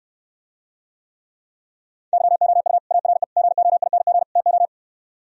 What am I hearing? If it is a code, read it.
Morse code, heard as a keyed tone, about 68 wpm, 700 Hz. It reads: YOU ARE GOING TO